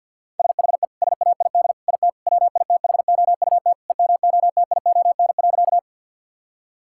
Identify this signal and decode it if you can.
Morse code, heard as a keyed tone, about 51 wpm, 700 Hz. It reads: SHE SAID IT WITHOUT EMOTION.
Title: Sentences from Top 1600 Words 50wpm